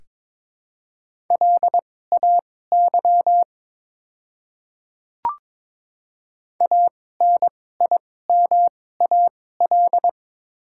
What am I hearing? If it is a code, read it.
Morse code, heard as a keyed tone, about 22 wpm, spaced as for 16 wpm, 700 Hz. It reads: LAY ANIMAL
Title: 2x Character Spacing - Top 500 Words - 22wpm